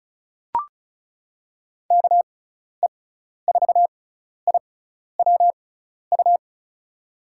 Morse code, heard as a keyed tone, about 34 wpm, spaced as for 12 wpm, 700 Hz. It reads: KE4IWU
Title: USA Call Signs - 6x Character Spacing - 35wpm